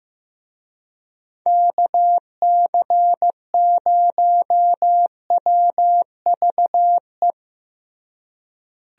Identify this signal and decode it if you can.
Morse code, heard as a keyed tone, about 15 wpm, 700 Hz. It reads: KC0WVE
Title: USA Call Signs - 6 Characters - Repeated 2x - 15wpm